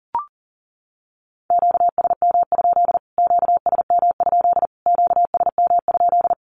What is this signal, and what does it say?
Morse code, heard as a keyed tone, about 40 wpm, 700 Hz. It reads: QSM? QSM? QSM?
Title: Q-Codes Repeated 3x 40wpm